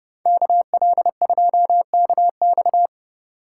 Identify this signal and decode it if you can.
Morse code, heard as a keyed tone, about 36 wpm, 700 Hz. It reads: KL2KX
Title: Speed Racing - USA 2x2 Call Signs 30wpm